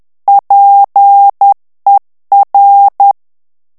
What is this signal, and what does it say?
Morse code, heard as a keyed tone, about 10 wpm, 800 Hz. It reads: PER
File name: morse mp3 / ITALIEN texte clair 1 vitesse 10 mots minute